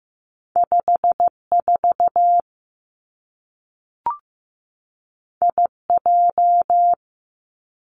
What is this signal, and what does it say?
Morse code, heard as a keyed tone, about 15 wpm, 700 Hz. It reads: 5 4 I J
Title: Two Challenging Letter-Number Combinations 15wpm